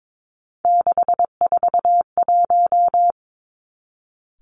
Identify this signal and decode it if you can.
Morse code, heard as a keyed tone, about 22 wpm, 700 Hz. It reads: 641
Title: Three Challenging Letter-Number Combinations 22wpm